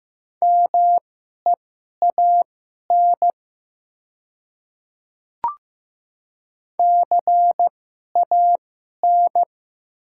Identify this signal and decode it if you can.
Morse code, heard as a keyed tone, about 15 wpm, spaced as for 11 wpm, 700 Hz. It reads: MEAN CAN
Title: 2x Character Spacing - Top 200 Words - 15wpm